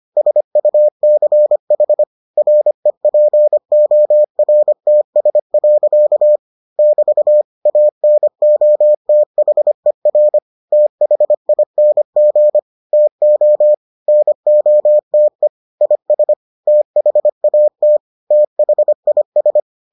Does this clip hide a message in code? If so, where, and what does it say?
Yes: Morse code from 0.2 to 19.6 s, heard as a keyed tone, about 25 wpm, 600 Hz. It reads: SUCH REPORTS. <BT> ANOTHER THING TO NOTE IS THAT THIS